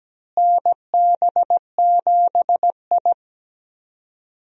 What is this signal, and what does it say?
Morse code, heard as a keyed tone, about 17 wpm, 700 Hz. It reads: NB7I
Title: USA Call Signs - 4 Characters - Repeated 3x - 17wpm